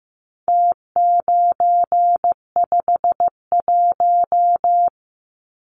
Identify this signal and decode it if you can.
Morse code, heard as a keyed tone, about 15 wpm, 700 Hz. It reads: T951
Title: Numbers - Contest Style - 4-Digit 15wpm